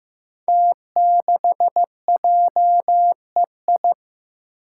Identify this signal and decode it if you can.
Morse code, heard as a keyed tone, about 15 wpm, 700 Hz. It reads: T6JEI
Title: Five Challenging Letter-Number Combinations 15wpm